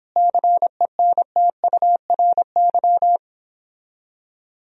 Morse code, heard as a keyed tone, about 29 wpm, 700 Hz. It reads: CENTURY